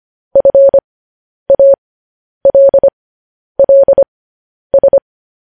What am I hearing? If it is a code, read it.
Morse code, heard as a keyed tone, about 25 wpm, spaced as for 10 wpm, 550 Hz. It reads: FALLS
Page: news-headlines-in-morse-code-at-10-wpm